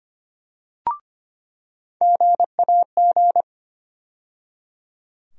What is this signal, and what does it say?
Morse code, heard as a keyed tone, about 25 wpm, 700 Hz. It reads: GAG
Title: USA Call Sign Postfixes - 3 Letters 25wpm